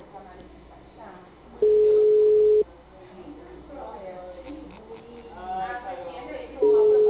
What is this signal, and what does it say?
Ringback tone (Europe, 425 Hz), with speech.